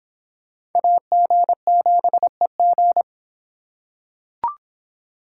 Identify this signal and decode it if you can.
Morse code, heard as a keyed tone, about 25 wpm, 700 Hz. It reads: AG7EG